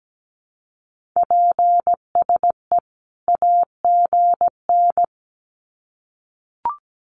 Morse code, heard as a keyed tone, about 17 wpm, 700 Hz. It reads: PSE AGN
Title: Common QSO Phrases 17wpm